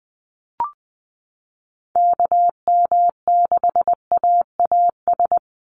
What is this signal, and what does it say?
Morse code, heard as a keyed tone, about 20 wpm, 700 Hz. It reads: KM6AAS
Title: USA Call Signs 20wpm